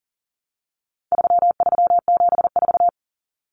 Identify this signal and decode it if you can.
Morse code, heard as a keyed tone, about 38 wpm, 700 Hz. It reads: 3374
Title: Numbers - 4-Digit 40wpm